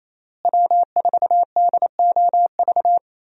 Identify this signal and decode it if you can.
Morse code, heard as a keyed tone, about 28 wpm, 700 Hz. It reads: W4DOV
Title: USA Call Signs - 5 Characters - Repeated 3x - 28wpm